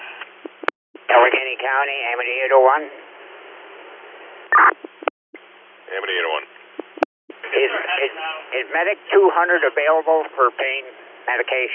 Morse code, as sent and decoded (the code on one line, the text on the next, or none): none